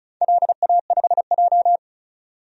Morse code, heard as a keyed tone, about 36 wpm, 690 Hz. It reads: LA5J
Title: DXCC-100 Call Signs - 4 Characters - 35wpm